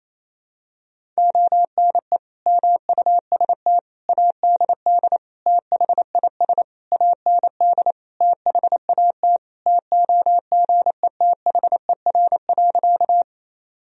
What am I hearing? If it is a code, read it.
Morse code, heard as a keyed tone, about 28 wpm, 700 Hz. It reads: ONE MUST ADD THIS AND THAT TOGETHER.
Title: Sentences from Top 1000 Words - Review Part 1 - 28wpm